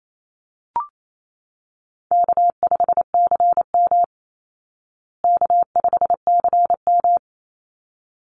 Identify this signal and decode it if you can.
Morse code, heard as a keyed tone, about 28 wpm, 700 Hz. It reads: K5CM K5CM
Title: Call Signs Repeated 3x - 28wpm